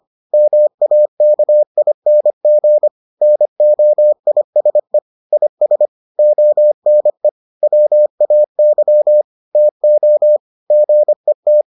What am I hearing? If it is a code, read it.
Morse code, heard as a keyed tone, about 25 wpm, 600 Hz. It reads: MAKING NOISE IS ONE WAY TO GET